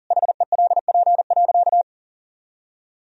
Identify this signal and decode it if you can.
Morse code, heard as a keyed tone, about 40 wpm, 700 Hz. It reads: HELP.
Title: Sentences from Top 1000 Words - Review Part 2 - 40wpm